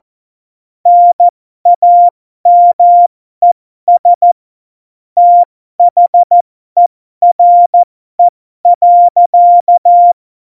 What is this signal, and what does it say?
Morse code, heard as a keyed tone, about 14 wpm, spaced as for 12 wpm, 700 Hz. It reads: NAMES THERE.
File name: Morsekurs / Witze